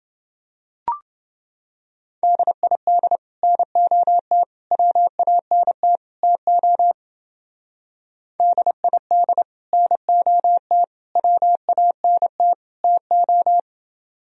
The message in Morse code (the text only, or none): DID NOT WANT TO DID NOT WANT TO